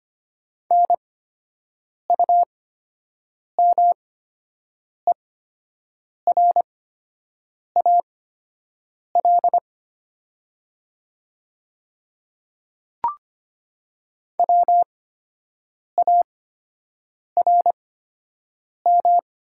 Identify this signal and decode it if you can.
Morse code, heard as a keyed tone, about 25 wpm, spaced as for 7 wpm, 700 Hz. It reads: NUMERAL WARM
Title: 8x Character Spacing - Top 500 Words - 25wpm